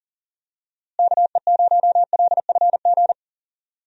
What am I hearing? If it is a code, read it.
Morse code, heard as a keyed tone, about 40 wpm, 700 Hz. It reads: KE0LFG
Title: USA Call Signs - 6 Characters - Repeated 2x - 40wpm